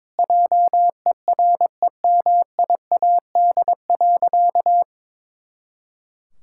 Morse code, heard as a keyed tone, about 22 wpm, 700 Hz. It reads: JEREMIAD.
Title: Pangram Challenge v2 - 22wpm